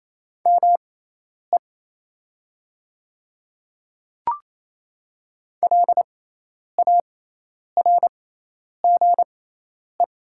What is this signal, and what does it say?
Morse code, heard as a keyed tone, about 28 wpm, spaced as for 10 wpm, 700 Hz. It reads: ME LARGE
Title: 6x Character Spacing - Top 300 Words - 28wpm